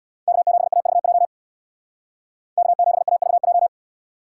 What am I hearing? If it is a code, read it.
Morse code, heard as a keyed tone, about 77 wpm, 700 Hz. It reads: K7AVJ K7AVJ